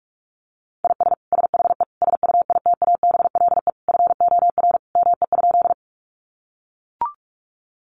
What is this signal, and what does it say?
Morse code, heard as a keyed tone, about 44 wpm, 700 Hz. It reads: IS SHE SUITABLE FOR ME?
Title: Sentences from Top 1400 Words 45wpm